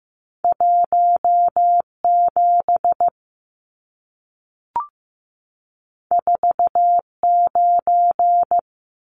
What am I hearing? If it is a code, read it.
Morse code, heard as a keyed tone, about 15 wpm, 700 Hz. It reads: 17 49
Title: Numbers - 2-Digit 15wpm